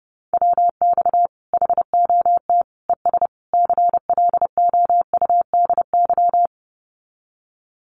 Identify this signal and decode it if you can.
Morse code, heard as a keyed tone, about 30 wpm, 700 Hz. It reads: WX HOT ES CLOUDY
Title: Common QSO Phrases 30wpm